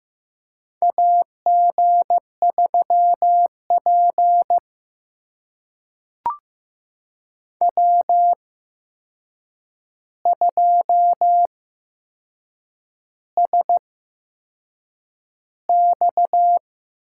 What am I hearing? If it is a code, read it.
Morse code, heard as a keyed tone, about 15 wpm, spaced as for 6 wpm, 700 Hz. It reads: AG3P W2SX
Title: USA Call Signs - 4 Characters - 8x Character Spacing - 15wpm